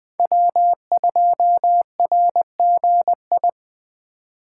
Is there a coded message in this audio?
Yes: Morse code from 0.2 to 3.5 s, heard as a keyed tone, about 20 wpm, 700 Hz. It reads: W2RGI